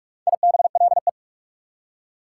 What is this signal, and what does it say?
Morse code, heard as a keyed tone, about 47 wpm, 690 Hz. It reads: IBLE
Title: Prefixes and Suffixes 45wpm